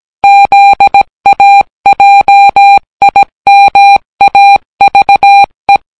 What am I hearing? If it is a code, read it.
Morse code, heard as a keyed tone, about 18 wpm, 800 Hz. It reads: ZAJIMAVE